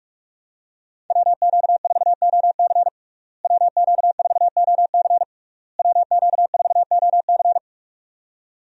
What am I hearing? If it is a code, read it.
Morse code, heard as a keyed tone, about 46 wpm, 700 Hz. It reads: WQ4OC WQ4OC WQ4OC
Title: USA Call Signs - 5 Characters - Repeated 3x - 45wpm